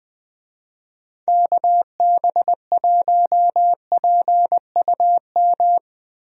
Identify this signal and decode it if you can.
Morse code, heard as a keyed tone, about 20 wpm, 700 Hz. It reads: KB1PUM
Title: USA Call Signs - 6 Characters - Repeated 2x - 20wpm